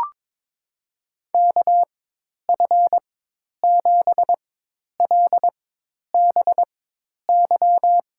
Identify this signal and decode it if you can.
Morse code, heard as a keyed tone, about 23 wpm, spaced as for 10 wpm, 700 Hz. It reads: KF7LBY
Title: USA Call Signs - 6 Characters - 4x Character Spacing - 22wpm